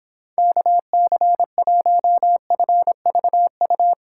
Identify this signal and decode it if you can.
Morse code, heard as a keyed tone, about 26 wpm, 700 Hz. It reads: KC1FVU